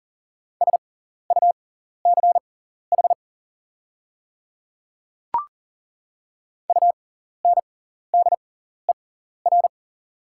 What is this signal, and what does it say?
Morse code, heard as a keyed tone, about 40 wpm, spaced as for 14 wpm, 700 Hz. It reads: SUCH UNDER
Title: 6x Character Spacing - Top 300 Words - 40wpm